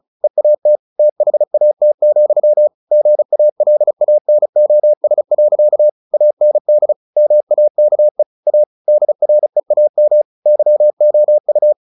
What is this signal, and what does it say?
Morse code, heard as a keyed tone, about 34 wpm, 600 Hz. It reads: EAT THAT, GALANOS. AND MAKE A DREAM YOU
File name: the old man and the sea 2 35wpm